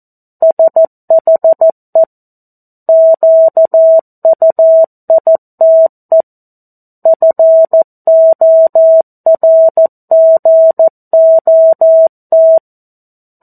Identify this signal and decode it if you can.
Morse code, heard as a keyed tone, about 14 wpm, 650 Hz. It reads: SHE QUITE FORGOT
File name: 02-Chapter-02 14 WPM